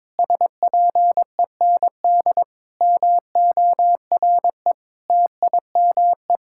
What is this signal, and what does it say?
Morse code, heard as a keyed tone, about 22 wpm, 700 Hz. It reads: SPEND MORE TIME
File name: Sentences with 7-Word Phrases - 22wpm